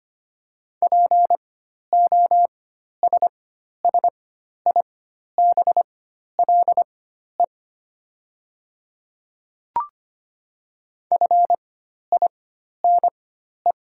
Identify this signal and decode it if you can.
Morse code, heard as a keyed tone, about 25 wpm, spaced as for 12 wpm, 700 Hz. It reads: POSSIBLE FINE
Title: 4x Character Spacing - Top 500 Words - 25wpm